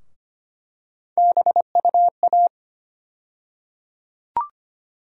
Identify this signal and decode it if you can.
Morse code, heard as a keyed tone, about 25 wpm, 700 Hz. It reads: BUA
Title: USA Call Sign Postfixes - 3 Letters 25wpm